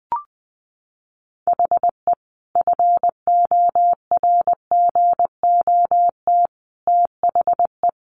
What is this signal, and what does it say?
Morse code, heard as a keyed tone, about 20 wpm, 700 Hz. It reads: HE FORGOT THE
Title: Sentences from Top 1400 Words 20wpm